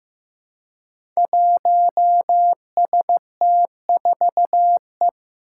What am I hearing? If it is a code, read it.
Morse code, heard as a keyed tone, about 15 wpm, 700 Hz. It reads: 1ST4E
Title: Five Challenging Letter-Number Combinations 15wpm